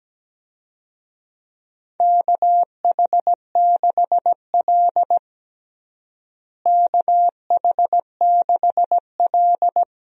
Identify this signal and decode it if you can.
Morse code, heard as a keyed tone, about 17 wpm, 700 Hz. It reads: KH6L KH6L
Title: DXCC-100 Call Signs - 4 Characters - Repeated 3x - 17wpm